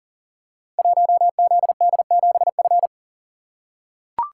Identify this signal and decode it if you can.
Morse code, heard as a keyed tone, about 39 wpm, 700 Hz. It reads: 1ZD7F
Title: Code Groups - 5 Letter:Numbers - 40wpm